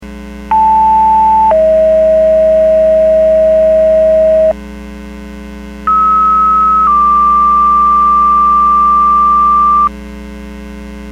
Two-tone page, heard as a one-second tone, then a three-second tone, about 880 then 640 Hz, about 1290 then 1180 Hz. It is heard from 0.5 to 9.9 s.